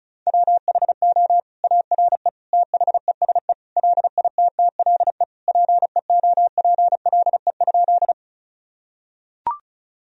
Morse code, heard as a keyed tone, about 36 wpm, 700 Hz. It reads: WHO ARE THESE LITTLE PEOPLE?
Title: Sentences from Top 1000 Words - Review Part 1 - 35wpm